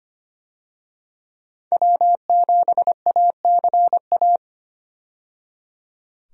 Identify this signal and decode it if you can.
Morse code, heard as a keyed tone, about 25 wpm, 700 Hz. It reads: W7ACA
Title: USA Call Signs - 5 Characters - 25wpm